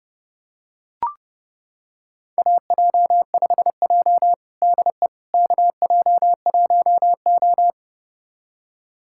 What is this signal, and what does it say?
Morse code, heard as a keyed tone, about 32 wpm, 700 Hz. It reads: AJ5J DE KJ1O
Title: USA Callsign DE Callsign - 4 Characters - 30wpm